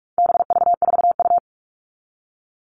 Morse code, heard as a keyed tone, about 44 wpm, 700 Hz. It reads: BV4U
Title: All Cut Numbers - 4-Digit 45wpm